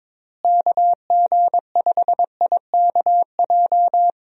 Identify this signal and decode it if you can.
Morse code, heard as a keyed tone, about 22 wpm, 700 Hz. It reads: KG5IKJ
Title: USA Call Signs - 6 Characters - Repeated 2x - 22wpm